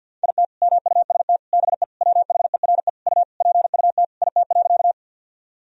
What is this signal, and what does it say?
Morse code, heard as a keyed tone, about 50 wpm, 700 Hz. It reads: IT MUST BE WHERE U PUT IT.
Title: Sentences from Top 200 Words and Abbreviations 50wpm